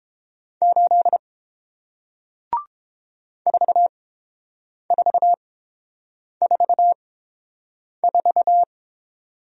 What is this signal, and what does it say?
Morse code, heard as a keyed tone, about 30 wpm, spaced as for 7 wpm, 700 Hz. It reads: 8 4444